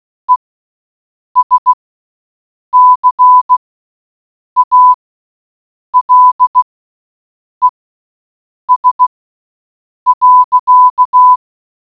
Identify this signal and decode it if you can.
Morse code, heard as a keyed tone, about 16 wpm, spaced as for 7 wpm, 1000 Hz. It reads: ESCALES.